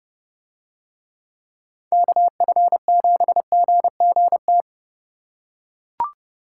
Morse code, heard as a keyed tone, about 30 wpm, 700 Hz. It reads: KF7GGT